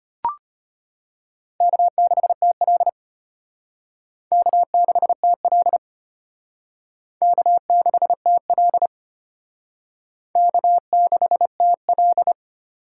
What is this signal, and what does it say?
Morse code, heard as a keyed tone, about 33 wpm, 700 Hz. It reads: K6TL K6TL K6TL K6TL